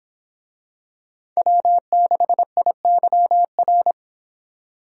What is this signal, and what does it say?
Morse code, heard as a keyed tone, about 26 wpm, 700 Hz. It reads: W6IYR